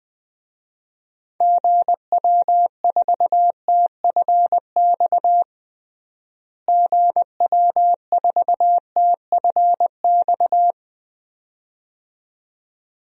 Morse code, heard as a keyed tone, about 21 wpm, 700 Hz. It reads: GW4TFX GW4TFX
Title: DXCC-100 Call Signs - 6 Characters - Repeated 2x - 20wpm